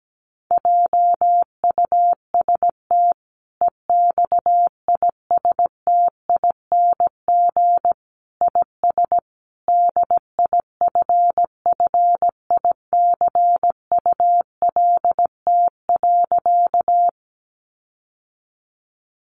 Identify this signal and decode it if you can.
Morse code, heard as a keyed tone, about 17 wpm, 700 Hz. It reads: JUST EXISTING IS DIFFICULT.
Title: Sentences from Top 1500 Words 17wpm